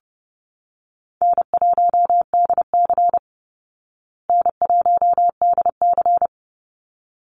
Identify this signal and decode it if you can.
Morse code, heard as a keyed tone, about 30 wpm, 700 Hz. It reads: N1DC N1DC
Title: Easy Call Signs Repeated 2x - 30wpm